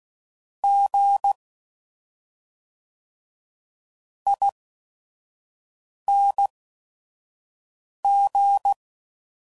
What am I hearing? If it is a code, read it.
Morse code, heard as a keyed tone, about 16 wpm, spaced as for 5 wpm, 800 Hz. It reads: G ING
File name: Suffixes / Special ING's / SlowToFaster ING's